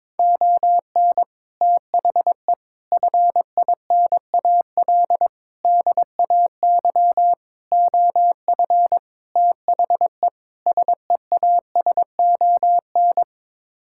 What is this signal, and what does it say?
Morse code, heard as a keyed tone, about 22 wpm, 700 Hz. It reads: ON THE FINAL DAY OF THE SEASON
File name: practice / 22wpm / Sets of 7 Words - 22wpm